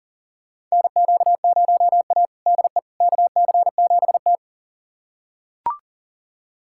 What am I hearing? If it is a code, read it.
Morse code, heard as a keyed tone, about 40 wpm, 700 Hz. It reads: NQ0A DE KC7T
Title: USA Callsign DE Callsign - 4 Characters - 40wpm